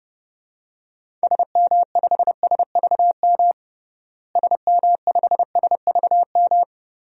Morse code, heard as a keyed tone, about 30 wpm, 700 Hz. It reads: SM5SVM SM5SVM
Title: DXCC-100 Call Signs - 6 Characters - Repeated 2x - 30wpm